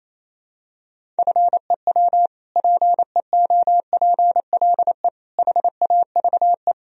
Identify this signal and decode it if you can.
Morse code, heard as a keyed tone, about 28 wpm, 700 Hz. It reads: FEW PEOPLE HAVE
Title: Sentences from Top 1000 Words - Review Part 1 - 28wpm